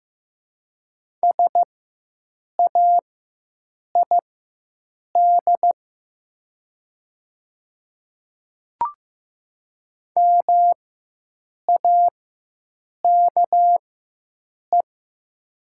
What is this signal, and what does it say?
Morse code, heard as a keyed tone, about 15 wpm, spaced as for 7 wpm, 700 Hz. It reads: SAID MAKE
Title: 4x Character Spacing - Top 200 Words - 15wpm